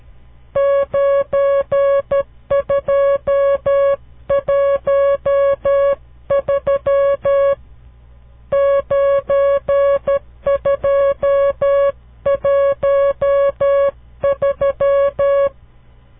Morse code, heard as a keyed tone, about 12 wpm, 560 Hz. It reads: 9213 9213
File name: CW Learning Programs / farnsworth / unit 5